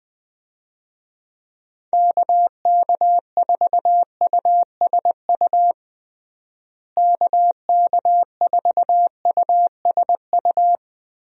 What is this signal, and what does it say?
Morse code, heard as a keyed tone, about 20 wpm, 700 Hz. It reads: KK4USU KK4USU
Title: USA Call Signs - 6 Characters - Repeated 3x - 20wpm